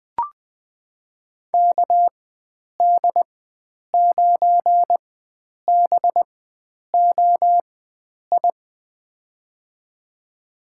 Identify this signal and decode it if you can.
Morse code, heard as a keyed tone, about 20 wpm, spaced as for 9 wpm, 700 Hz. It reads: KD9BOI
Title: USA Call Signs - 6 Characters - 4x Character Spacing - 20wpm